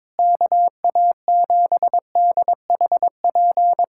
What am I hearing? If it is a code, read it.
Morse code, heard as a keyed tone, about 22 wpm, 700 Hz. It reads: KA7DHP
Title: USA Callsign DE Callsign 22wpm